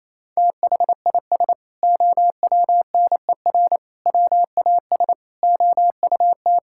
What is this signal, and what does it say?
Morse code, heard as a keyed tone, about 28 wpm, 700 Hz. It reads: THIS OWNER WAS OUT
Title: Sentences from Top 1600 Words 28wpm